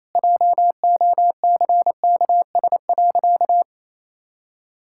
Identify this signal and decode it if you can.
Morse code, heard as a keyed tone, about 27 wpm, 700 Hz. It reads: JOCKS.
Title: Pangram Challenge v2 - 28wpm